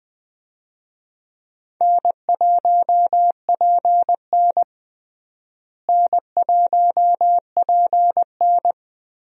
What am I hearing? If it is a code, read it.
Morse code, heard as a keyed tone, about 20 wpm, 700 Hz. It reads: N1PN N1PN
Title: USA Call Signs - 4 Characters - Repeated 2x - 20wpm